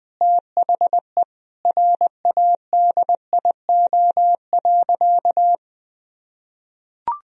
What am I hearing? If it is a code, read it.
Morse code, heard as a keyed tone, about 21 wpm, 700 Hz. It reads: THE RADIO.
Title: Sentences from Top 1000 Words - Review Part 3 - 20wpm